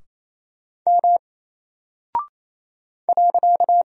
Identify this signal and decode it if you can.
Morse code, heard as a keyed tone, about 27 wpm, 700 Hz. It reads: M .
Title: Single Character (Letter:Number:Punctuation:Prosign) Alt 28wpm